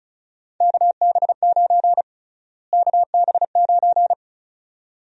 Morse code, heard as a keyed tone, about 36 wpm, 700 Hz. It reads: KB9 KB9
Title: USA Call Sign Prefixes - Repeated 2x - 35wpm